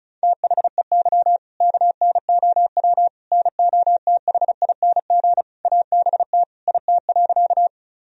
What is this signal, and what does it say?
Morse code, heard as a keyed tone, about 34 wpm, 700 Hz. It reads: THEY KNOW NOTHING ABT IT.
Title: Sentences from Top 500 Words and Abbreviations 35wpm